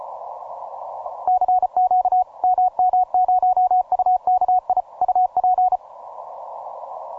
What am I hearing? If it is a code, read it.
Morse code, heard as a keyed tone, about 33 wpm, 740 Hz. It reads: CQ MM0UKI UP
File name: simple-file-list / MM0UKI EU189 ROCKALL 20MCW